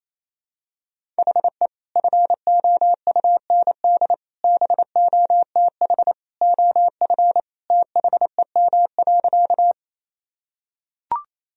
Morse code, heard as a keyed tone, about 28 wpm, 700 Hz. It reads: HE FOUND BOTH OF THEM.
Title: Sentences from Top 1000 Words - Review Part 1 - 28wpm